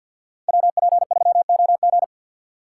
Morse code, heard as a keyed tone, about 52 wpm, 700 Hz. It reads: WP3OG
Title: USA Call Signs - 5 Characters - 3x Character Spacing - 50wpm